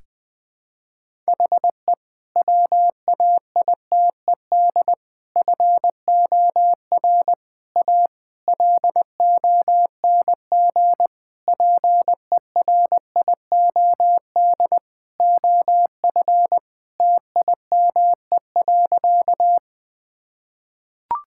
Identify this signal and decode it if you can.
Morse code, heard as a keyed tone, about 20 wpm, 700 Hz. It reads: HE WAITED FOR A LONG PERIOD OF TIME.
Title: Sentences with 5-Word Phrases - 20wpm